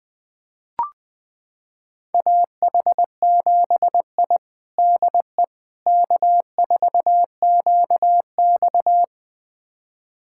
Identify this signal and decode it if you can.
Morse code, heard as a keyed tone, about 21 wpm, 700 Hz. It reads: AH7I DE K4QX
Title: USA Callsign DE Callsign - 4 Characters - 20wpm